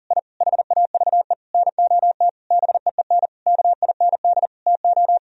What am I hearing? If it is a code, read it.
Morse code, heard as a keyed tone, about 37 wpm, 700 Hz. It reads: I HAVE NOT BEEN KIND TO
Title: Sentences from Top 1000 Words - Review Part 2 - 40wpm